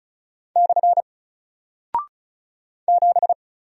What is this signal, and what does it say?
Morse code, heard as a keyed tone, about 36 wpm, 700 Hz. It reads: / 7
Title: Single Character (Letter:Number:Punctuation:Prosign) 35wpm